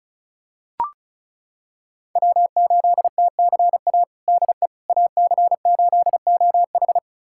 Morse code, heard as a keyed tone, about 36 wpm, 700 Hz. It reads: W8TCA DE AC8OH
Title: USA Callsign DE Callsign - 5 Characters - 35wpm